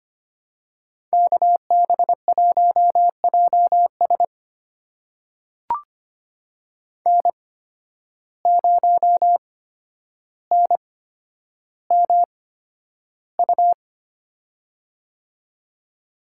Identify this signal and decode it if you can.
Morse code, heard as a keyed tone, about 25 wpm, spaced as for 11 wpm, 700 Hz. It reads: KB1JS N0NMU
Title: USA Call Signs - 5 Characters - 8x Character Spacing - 25wpm